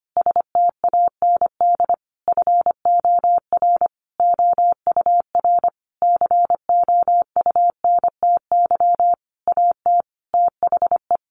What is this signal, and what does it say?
Morse code, heard as a keyed tone, about 25 wpm, 700 Hz. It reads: STAND FOR OUR COUNTY AT THE
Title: Sentences from Top 1500 Words 25wpm